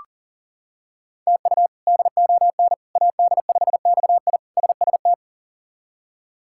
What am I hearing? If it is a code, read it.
Morse code, heard as a keyed tone, about 40 wpm, 700 Hz. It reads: TU DON AD5XI SST
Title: K1USN SST QSO Practice - 40wpm